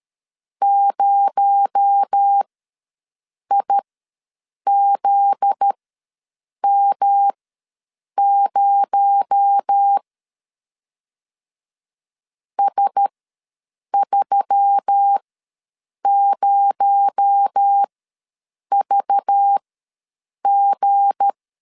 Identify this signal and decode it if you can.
Morse code, heard as a keyed tone, about 13 wpm, spaced as for 7 wpm, 800 Hz. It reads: 0IZM0 S30VG
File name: LEZIONI / 7 GF9Z / 7 GF9Z ws 12wpm 3321